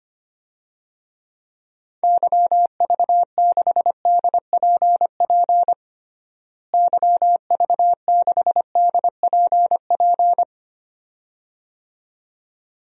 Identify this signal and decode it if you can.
Morse code, heard as a keyed tone, about 25 wpm, 700 Hz. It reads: YV6DPP YV6DPP
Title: DXCC-100 Call Signs - 6 Characters - Repeated 2x - 25wpm